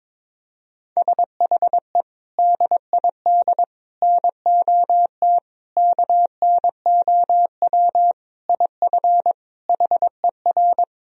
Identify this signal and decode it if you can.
Morse code, heard as a keyed tone, about 22 wpm, 700 Hz. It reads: SHE DID NOT KNOW IF HER
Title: Sentences from Top 1000 Words - Review Part 4 - 22wpm